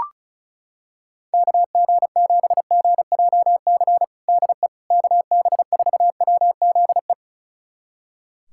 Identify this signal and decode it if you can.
Morse code, heard as a keyed tone, about 34 wpm, 700 Hz. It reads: KG7GJC DE KB4WZE
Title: USA Callsign DE Callsign 35wpm